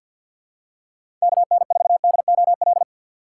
Morse code, heard as a keyed tone, about 52 wpm, 700 Hz. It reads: KN4DOL
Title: USA Call Signs - 6 Characters - 3x Character Spacing - 50wpm